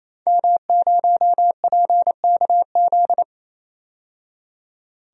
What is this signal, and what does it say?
Morse code, heard as a keyed tone, about 28 wpm, 700 Hz. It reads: M0PKZ
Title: DXCC-100 Call Signs - 5 Characters - 28wpm